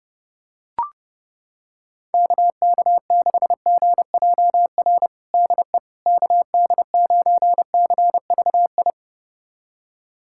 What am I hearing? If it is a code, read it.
Morse code, heard as a keyed tone, about 32 wpm, 700 Hz. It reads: KK6GJR DE KD9CVI
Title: USA Callsign DE Callsign - 6 Characters - 30wpm